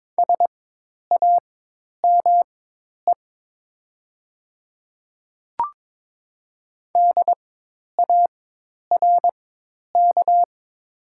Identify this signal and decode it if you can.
Morse code, heard as a keyed tone, about 22 wpm, spaced as for 10 wpm, 700 Hz. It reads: SAME DARK
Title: 4x Character Spacing - Top 500 Words - 22wpm